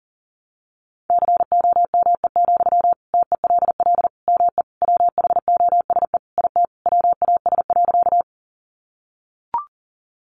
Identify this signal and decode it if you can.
Morse code, heard as a keyed tone, about 40 wpm, 700 Hz. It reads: COME, TELL ME WHOSE IT WAS.
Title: Sentences from Top 1200 Words 40wpm